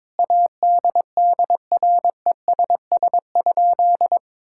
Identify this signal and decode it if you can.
Morse code, heard as a keyed tone, about 21 wpm, 700 Hz. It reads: ADDRESS?
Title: Sentences from Top 1000 Words - Review Part 6 - 22wpm